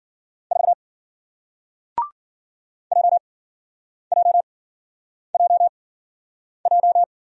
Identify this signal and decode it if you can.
Morse code, heard as a keyed tone, about 54 wpm, spaced as for 9 wpm, 700 Hz. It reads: 4 JJJJ